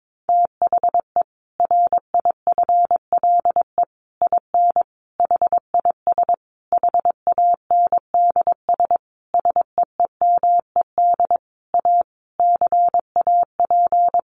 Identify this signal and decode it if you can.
Morse code, heard as a keyed tone, about 22 wpm, 700 Hz. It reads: THE RIFLE IN HIS HANDS SEEMED A CAP
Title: A Sound Of Thunder 22wpm